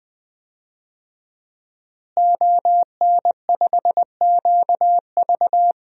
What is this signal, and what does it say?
Morse code, heard as a keyed tone, about 20 wpm, 700 Hz. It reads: ON5QV
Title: DXCC-100 Call Signs - 5 Characters - Repeated 2x - 20wpm